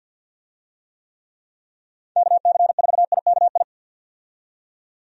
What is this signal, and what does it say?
Morse code, heard as a keyed tone, about 46 wpm, 700 Hz. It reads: KC4IKI
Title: USA Call Signs - 6 Characters - 50wpm